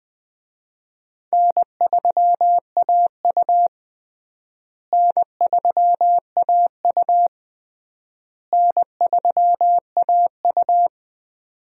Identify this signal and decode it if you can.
Morse code, heard as a keyed tone, about 20 wpm, 700 Hz. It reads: N3AU N3AU N3AU
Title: USA Call Signs - 4 Characters - Repeated 3x - 20wpm